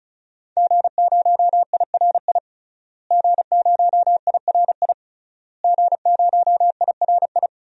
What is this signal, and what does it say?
Morse code, heard as a keyed tone, about 36 wpm, 700 Hz. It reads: G0IRI G0IRI G0IRI
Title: DXCC-100 Call Signs - 5 Characters - Repeated 3x - 35wpm